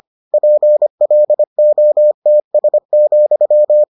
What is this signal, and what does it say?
Morse code, heard as a keyed tone, about 25 wpm, 600 Hz. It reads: PLOTS,